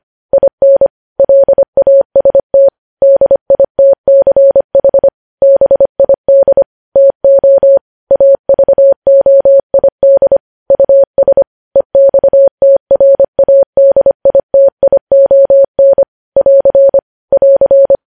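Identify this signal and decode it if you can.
Morse code, heard as a keyed tone, about 25 wpm, 550 Hz. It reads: IN LAST DITCH BID TO AVOID US EXTRADITION <AR> <AR>
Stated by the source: Morse code news headlines sent at 25 wpm.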